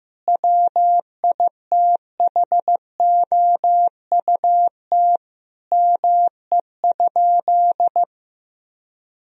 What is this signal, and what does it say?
Morse code, heard as a keyed tone, about 15 wpm, 700 Hz. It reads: WITHOUT ME?
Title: Sentences from Top 1000 Words - Review Part 3 - 15wpm